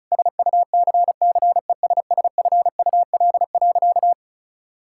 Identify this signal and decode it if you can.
Morse code, heard as a keyed tone, about 34 wpm, 700 Hz. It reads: SUCCESSFUL.
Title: Sentences from Top 1000 Words - Review Part 6 - 35wpm